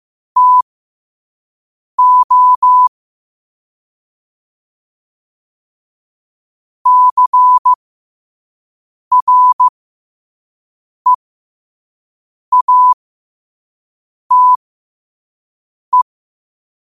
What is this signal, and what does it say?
Morse code, heard as a keyed tone, about 15 wpm, spaced as for 5 wpm, 1000 Hz. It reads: TO CREATE
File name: qotdfiles / 5wpm